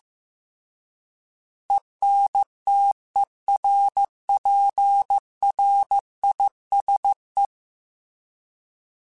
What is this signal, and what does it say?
Morse code, heard as a keyed tone, about 15 wpm, 800 Hz. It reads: ENTERPRISE
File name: fromSMs / FromSMs mp3s / Cities@13